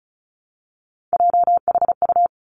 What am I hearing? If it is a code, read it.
Morse code, heard as a keyed tone, about 36 wpm, 700 Hz. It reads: JHU
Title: Three Challenging Letter-Number Combinations 35wpm